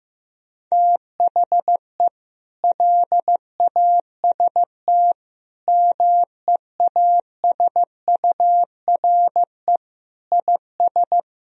Morse code, heard as a keyed tone, about 15 wpm, 700 Hz. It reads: THE LAST MEASURE IS
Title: Sentences from Top 1000 Words - Review Part 2 - 15wpm